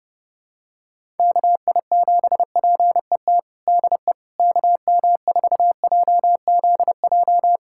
Morse code, heard as a keyed tone, about 28 wpm, 700 Hz. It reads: KI7PET DE KM4JZJ
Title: USA Callsign DE Callsign - 6 Characters - 30wpm